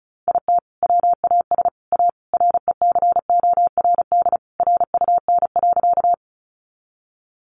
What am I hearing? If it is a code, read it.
Morse code, heard as a keyed tone, about 34 wpm, 700 Hz. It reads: IT WAS A RECORD RUN.
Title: Sentences from Top 500 Words 35wpm